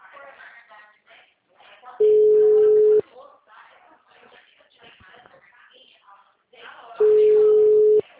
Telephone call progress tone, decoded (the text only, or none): ringback tone